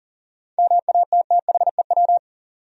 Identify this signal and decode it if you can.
Morse code, heard as a keyed tone, about 40 wpm, 700 Hz. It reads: MATTHEW
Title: USA Common Names - 40wpm